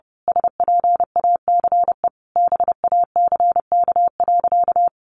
Morse code, heard as a keyed tone, about 30 wpm, 700 Hz. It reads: SPACE BACK.
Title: Sentences from Top 1500 Words 30wpm